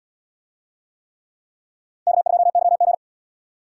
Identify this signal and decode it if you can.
Morse code, heard as a keyed tone, about 77 wpm, 700 Hz. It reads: K2YG